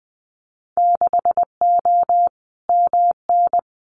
Morse code, heard as a keyed tone, about 20 wpm, 700 Hz. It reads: 6O MN
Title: Winter Field Day Exchanges Encore 20wpm